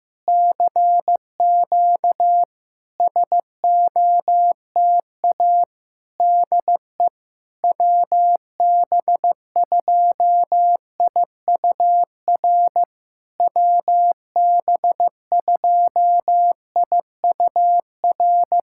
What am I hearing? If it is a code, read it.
Morse code, heard as a keyed tone, about 16 wpm, 700 Hz. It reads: CQ SOTA DE WB2IUR WB2IUR
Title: SOTA QSO Practice 15wpm